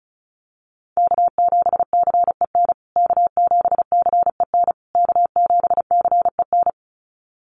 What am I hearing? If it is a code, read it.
Morse code, heard as a keyed tone, about 34 wpm, 700 Hz. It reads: K7CEN K7CEN K7CEN
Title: USA Call Signs Repeated 3x 35wpm